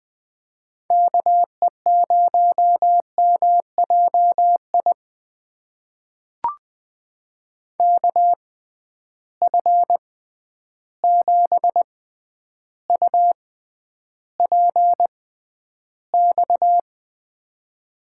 Morse code, heard as a keyed tone, about 21 wpm, spaced as for 10 wpm, 700 Hz. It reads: KE0MJI KF7UPX
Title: USA Call Signs - 6 Characters - 6x Character Spacing - 20wpm